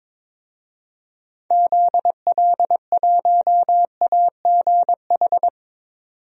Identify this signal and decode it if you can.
Morse code, heard as a keyed tone, about 22 wpm, 700 Hz. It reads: ZL1AGH
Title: DXCC-100 Call Signs - 6 Characters - 22wpm